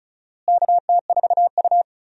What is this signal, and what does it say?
Morse code, heard as a keyed tone, about 36 wpm, 700 Hz. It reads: KT4U
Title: USA Call Signs - 4 Characters - Repeated 3x - 35wpm